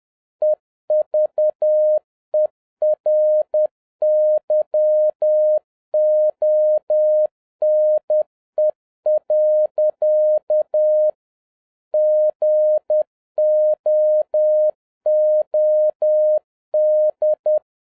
Morse code, heard as a keyed tone, about 10 wpm, 610 Hz. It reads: EVERYONE. GOOD